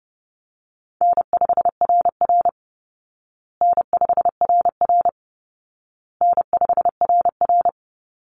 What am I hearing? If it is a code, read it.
Morse code, heard as a keyed tone, about 30 wpm, 700 Hz. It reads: N5RR N5RR N5RR
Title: Call Signs Repeated 2x - 30wpm